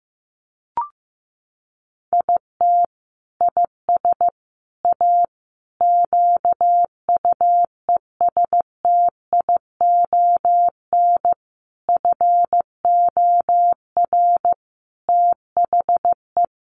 Morse code, heard as a keyed tone, about 15 wpm, 700 Hz. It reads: IT IS A QUESTION FOR THE
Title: Sentences from Top 1300 Words 15wpm